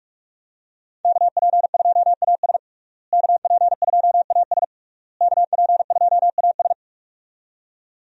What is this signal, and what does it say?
Morse code, heard as a keyed tone, about 42 wpm, 700 Hz. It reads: KP2AS KP2AS KP2AS
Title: USA Call Signs - 5 Characters - Repeated 3x - 45wpm